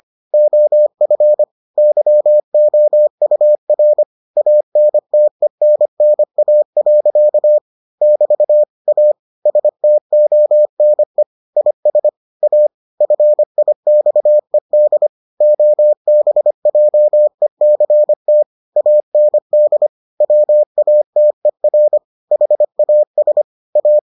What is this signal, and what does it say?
Morse code, heard as a keyed tone, about 25 wpm, 600 Hz. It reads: OF YOUR ANTENNA. <BT> A STONE IS A FIXED OBJECT AND WATER HAS A